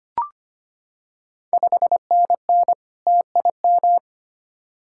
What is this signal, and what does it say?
Morse code, heard as a keyed tone, about 25 wpm, 700 Hz. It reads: 5NN TIM
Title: Ontario QSO Party Exchange 25wpm